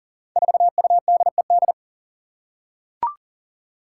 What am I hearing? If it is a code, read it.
Morse code, heard as a keyed tone, about 43 wpm, 700 Hz. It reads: 4UDED
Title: Five Challenging Letter-Number Combinations 40wpm